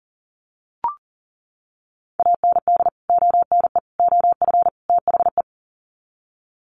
Morse code, heard as a keyed tone, about 40 wpm, 700 Hz. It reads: AND ONE OF THE
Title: Sets of 4 Words 40wpm